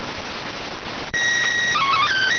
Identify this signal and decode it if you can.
SSTV (slow-scan television), PD 290.